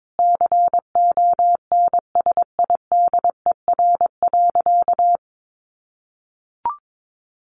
Morse code, heard as a keyed tone, about 22 wpm, 700 Hz. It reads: CONSIDER.
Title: Sentences from Top 900 Words 22wpm